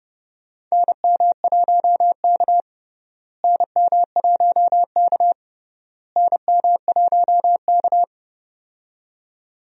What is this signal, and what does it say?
Morse code, heard as a keyed tone, about 32 wpm, 700 Hz. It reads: NM1K NM1K NM1K
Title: USA Call Signs - 4 Characters - Repeated 3x - 30wpm